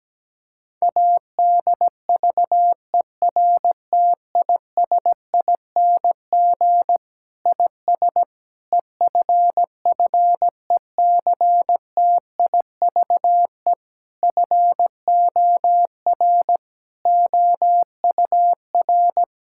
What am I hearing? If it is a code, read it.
Morse code, heard as a keyed tone, about 17 wpm, 700 Hz. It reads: ADVERTISING IS EFFECTIVE FOR OUR
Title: Sentences from Top 1000 Words - Review Part 6 - 17wpm